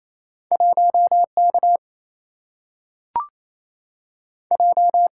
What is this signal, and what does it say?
Morse code, heard as a keyed tone, about 28 wpm, 700 Hz. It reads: 1K J
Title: Single-Letter Single-Number 28wpm